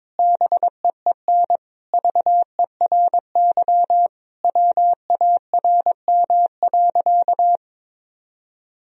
Morse code, heard as a keyed tone, about 22 wpm, 700 Hz. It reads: BEEN VERY WARM.
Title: Sentences from Top 1000 Words - Review Part 2 - 22wpm